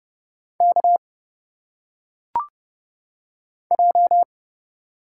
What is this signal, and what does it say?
Morse code, heard as a keyed tone, about 30 wpm, 700 Hz. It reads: K J
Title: Single Letters - Phonetic Pronunciation 30wpm